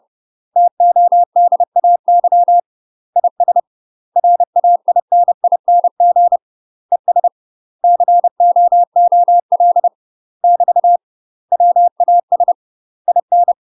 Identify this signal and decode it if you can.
Morse code, heard as a keyed tone, about 30 wpm, 700 Hz. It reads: TODAY IS RAINING ES COOL <BT> WAS IN
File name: UK MASTER QSOs TRAINING/30WPM